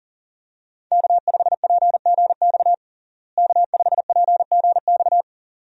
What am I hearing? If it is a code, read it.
Morse code, heard as a keyed tone, about 40 wpm, 700 Hz. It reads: K5PGX K5PGX
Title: USA Call Signs - 5 Characters - Repeated 2x - 40wpm